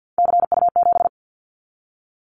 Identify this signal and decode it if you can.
Morse code, heard as a keyed tone, about 50 wpm, 690 Hz. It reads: CU7
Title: Callsign Prefixes for Top 100 DXCC - 50wpm